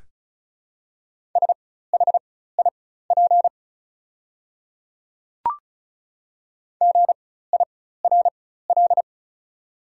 Morse code, heard as a keyed tone, about 36 wpm, spaced as for 16 wpm, 700 Hz. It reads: SHIP GIRL
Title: 4x Character Spacing - Top 500 Words - 35wpm